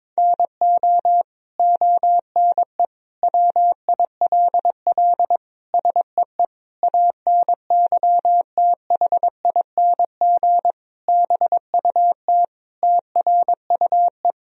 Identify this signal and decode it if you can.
Morse code, heard as a keyed tone, about 22 wpm, 700 Hz. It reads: NO ONE WILL SEE ANYTHING BUT TRUE
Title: Sentences from Top 1000 Words - Review Part 3 - 22wpm